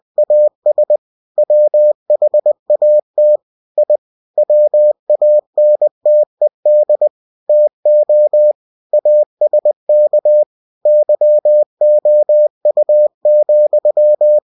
Morse code, heard as a keyed tone, about 20 wpm, 600 Hz. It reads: AS WHAT I WANTED TO ASK YOU,
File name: cw-books / winnie-the-pooh / winnie-20wpm-page-0051